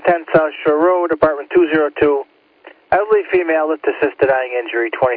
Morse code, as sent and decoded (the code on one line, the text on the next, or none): none